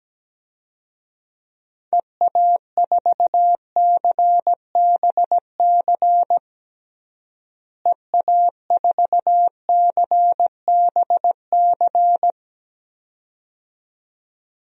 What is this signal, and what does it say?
Morse code, heard as a keyed tone, about 17 wpm, 700 Hz. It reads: EA4CBC EA4CBC
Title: DXCC-100 Call Signs - 6 Characters - Repeated 2x - 17wpm